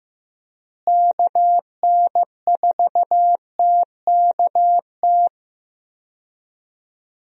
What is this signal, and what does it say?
Morse code, heard as a keyed tone, about 15 wpm, 700 Hz. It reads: KN4TKT
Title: USA Call Signs - 6 Characters - Repeated 3x - 15wpm